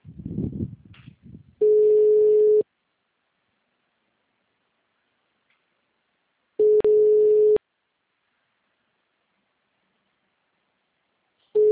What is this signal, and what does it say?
Ringback tone (Europe, 425 Hz).
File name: out- -2037- - - .9965